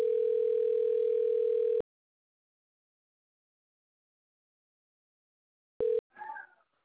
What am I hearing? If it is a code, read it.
Ringback tone (North America), 440 + 480 Hz.